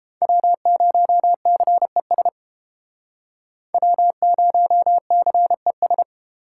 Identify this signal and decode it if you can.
Morse code, heard as a keyed tone, about 31 wpm, 700 Hz. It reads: W0CES W0CES